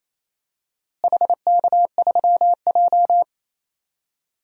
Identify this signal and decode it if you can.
Morse code, heard as a keyed tone, about 28 wpm, 700 Hz. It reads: HK3J
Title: DXCC-100 Call Signs - 4 Characters - Repeated 3x - 28wpm